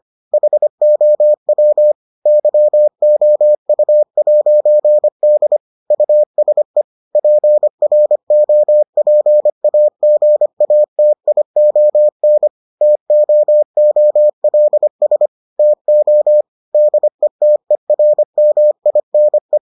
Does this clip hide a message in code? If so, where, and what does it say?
Yes: Morse code from 0.3 to 19.6 s, heard as a keyed tone, about 25 wpm, 600 Hz. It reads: HOW YOU'D USE PROPAGATION TOOLS TO DETERMINE